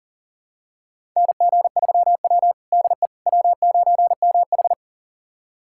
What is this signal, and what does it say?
Morse code, heard as a keyed tone, about 37 wpm, 700 Hz. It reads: NG3W DE W9MH
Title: USA Callsign DE Callsign - 4 Characters - 40wpm